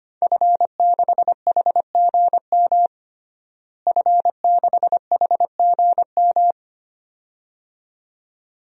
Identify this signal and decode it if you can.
Morse code, heard as a keyed tone, about 25 wpm, 700 Hz. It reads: F6HGM F6HGM
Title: DXCC-100 Call Signs - 5 Characters - Repeated 2x - 25wpm